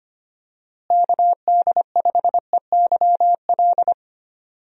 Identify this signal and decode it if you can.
Morse code, heard as a keyed tone, about 25 wpm, 700 Hz. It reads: KD5EYL
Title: USA Call Signs - 6 Characters - Repeated 3x - 25wpm